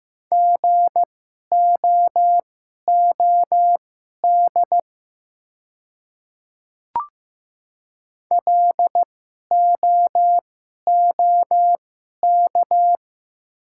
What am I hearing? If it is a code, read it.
Morse code, heard as a keyed tone, about 15 wpm, spaced as for 11 wpm, 700 Hz. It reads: GOOD LOOK
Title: 2x Character Spacing - Top 200 Words - 15wpm